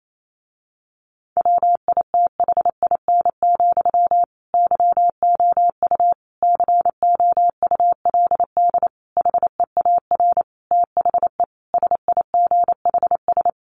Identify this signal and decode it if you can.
Morse code, heard as a keyed tone, about 28 wpm, 700 Hz. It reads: WITHIN, YOU COULD HEAR THE SIGHS
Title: A Sound Of Thunder 28wpm